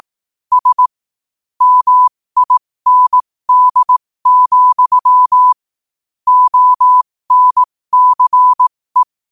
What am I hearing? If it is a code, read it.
Morse code, heard as a keyed tone, about 18 wpm, spaced as for 15 wpm, 1000 Hz. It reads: S MIND, ONCE
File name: qotdfiles / 15wpm